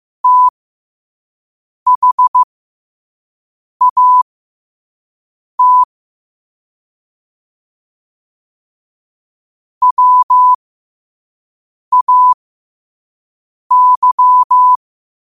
Morse code, heard as a keyed tone, about 15 wpm, spaced as for 5 wpm, 1000 Hz. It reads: THAT WAY
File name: 5wpm